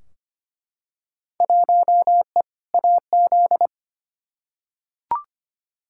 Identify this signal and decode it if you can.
Morse code, heard as a keyed tone, about 25 wpm, 700 Hz. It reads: 1E AZ
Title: Field Day Exchanges - 25wpm